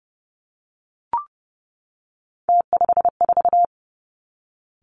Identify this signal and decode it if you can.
Morse code, heard as a keyed tone, about 30 wpm, 700 Hz. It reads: T54
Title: Three Challenging Letter-Number Combinations 30wpm